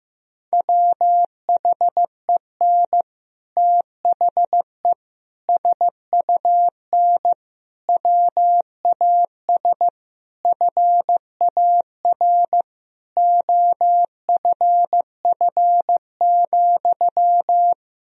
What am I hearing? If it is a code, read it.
Morse code, heard as a keyed tone, about 15 wpm, 700 Hz. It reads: WHEN THE SUN WAS FAR OFF,
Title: Sentences from Top 1000 Words - Review Part 3 - 15wpm